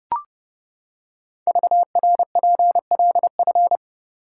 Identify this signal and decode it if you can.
Morse code, heard as a keyed tone, about 30 wpm, 700 Hz. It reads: VRPLF
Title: Code Groups - 5 Letters - 30wpm